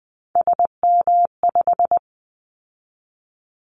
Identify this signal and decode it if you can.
Morse code, heard as a keyed tone, about 20 wpm, 700 Hz. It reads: SM5
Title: Callsign Prefixes for Top 25 DXCC - 20wpm